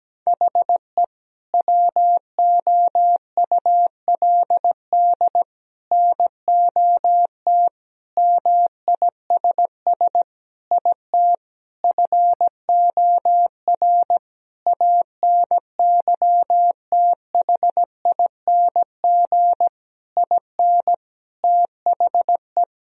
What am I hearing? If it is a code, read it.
Morse code, heard as a keyed tone, about 17 wpm, 700 Hz. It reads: HE WOULD NOT MISS IT FOR ANYTHING IN THE
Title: Sentences from Top 1000 Words - Review Part 3 - 17wpm